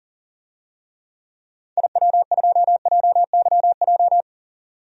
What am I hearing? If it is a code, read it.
Morse code, heard as a keyed tone, about 40 wpm, 700 Hz. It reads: IW2JYJ
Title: DXCC-100 Call Signs - 6 Characters - 40wpm